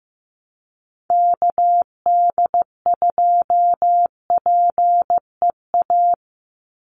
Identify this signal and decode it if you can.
Morse code, heard as a keyed tone, about 15 wpm, 700 Hz. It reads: KD2PEA
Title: USA Call Signs 15wpm